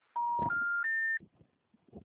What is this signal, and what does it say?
Special information tone.